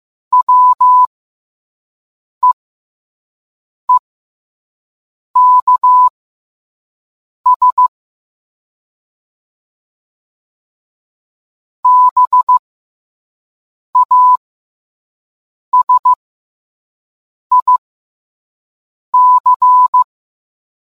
Morse code, heard as a keyed tone, about 15 wpm, spaced as for 5 wpm, 1000 Hz. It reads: WEEKS BASIC